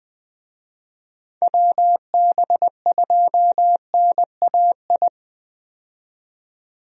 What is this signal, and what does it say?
Morse code, heard as a keyed tone, about 20 wpm, 700 Hz. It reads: WB2NAI
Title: USA Call Signs - 6 Characters - 20wpm